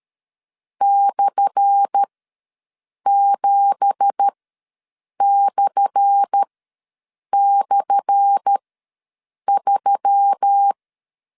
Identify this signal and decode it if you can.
Morse code, heard as a keyed tone, about 13 wpm, spaced as for 7 wpm, 800 Hz. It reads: /7//3
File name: LEZIONI / 10 SIGNS / 10A SLASH ws 12wpm 3443